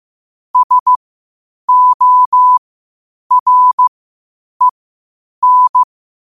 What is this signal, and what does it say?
Morse code, heard as a keyed tone, about 15 wpm, spaced as for 8 wpm, 1000 Hz. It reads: SOREN
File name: qotdfiles / 8wpm